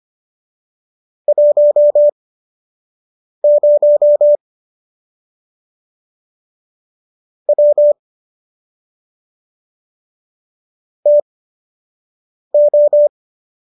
Morse code, heard as a keyed tone, about 25 wpm, spaced as for 6 wpm, 590 Hz. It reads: 10 W TO